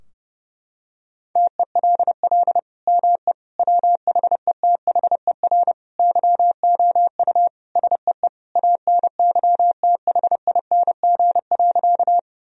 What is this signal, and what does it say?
Morse code, heard as a keyed tone, about 30 wpm, 700 Hz. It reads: TELL ME WHETHER YOU SEE ANYTHING.
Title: Sentences from Top 1000 Words - Review Part 3 - 30wpm